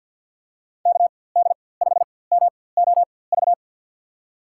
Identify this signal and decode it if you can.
Morse code, heard as a keyed tone, about 50 wpm, spaced as for 23 wpm, 700 Hz. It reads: KD5MOV
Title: USA Call Signs - 6 Characters - 4x Character Spacing - 50wpm